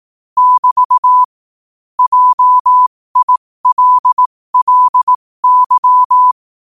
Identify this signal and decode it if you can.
Morse code, heard as a keyed tone, about 18 wpm, spaced as for 15 wpm, 1000 Hz. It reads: <BT> JILLY
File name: qotdfiles / 15wpm